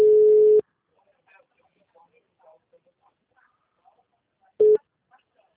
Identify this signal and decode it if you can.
Ringback tone (Europe, 425 Hz), with speech.